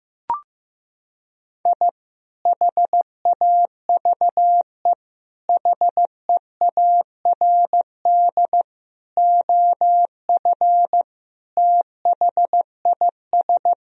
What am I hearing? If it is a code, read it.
Morse code, heard as a keyed tone, about 15 wpm, 700 Hz. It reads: I HAVE HEARD OF THIS
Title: Sentences from Top 1000 Words - Review Part 1 - 15wpm